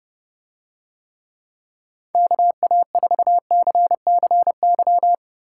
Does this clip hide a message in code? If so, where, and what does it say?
Yes: Morse code from 2.1 to 5.1 s, heard as a keyed tone, about 31 wpm, 700 Hz. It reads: KA4CCY